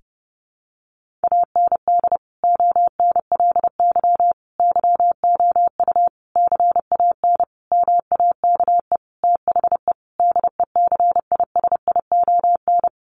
Morse code, heard as a keyed tone, about 30 wpm, 700 Hz. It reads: AND ONLY YOU CAN MAKE THE DECISION
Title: Sentences from Top 1400 Words 30wpm